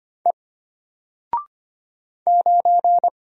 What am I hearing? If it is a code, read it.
Morse code, heard as a keyed tone, about 25 wpm, 700 Hz. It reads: E 9
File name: Single-Letter-Number-25wpm